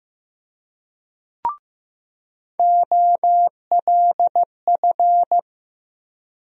Morse code, heard as a keyed tone, about 15 wpm, 700 Hz. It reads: OLF
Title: USA Call Sign Postfixes - 3 Letters 15wpm